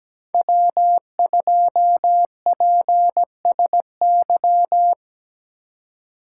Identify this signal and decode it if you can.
Morse code, heard as a keyed tone, about 19 wpm, 700 Hz. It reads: W2PSY
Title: Speed Racing - USA 1x3 Call Signs 17wpm